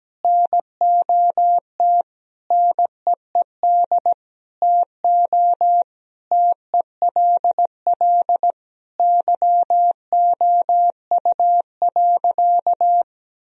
Morse code, heard as a keyed tone, about 17 wpm, 700 Hz. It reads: NOT NEED TO TELL YOU.
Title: Sentences from Top 1000 Words - Review Part 1 - 17wpm